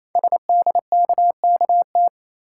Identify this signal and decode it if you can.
Morse code, heard as a keyed tone, about 28 wpm, 700 Hz. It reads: SDKKT
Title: Code Groups - 5 Letters - 28wpm